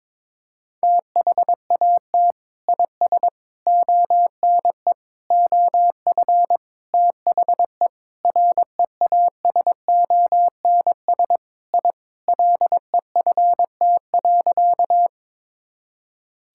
Morse code, heard as a keyed tone, about 22 wpm, 700 Hz. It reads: THAT IS ONE OF THE REASONS I LEFT.
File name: practice / 22wpm / Sentences with 4-Word Phrases - 22wpm